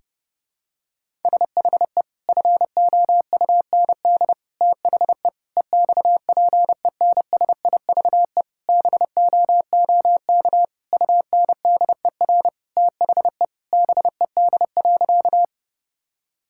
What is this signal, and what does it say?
Morse code, heard as a keyed tone, about 30 wpm, 700 Hz. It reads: SHE FOUND THE EXPENSIVE BOOK UNDER THE BED.
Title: Sentences from Top 1000 Words - Review Part 6 - 30wpm